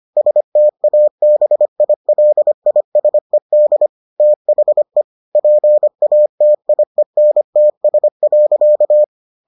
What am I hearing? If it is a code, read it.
Morse code, heard as a keyed tone, about 25 wpm, 600 Hz. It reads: STABILISED THE PATIENTS.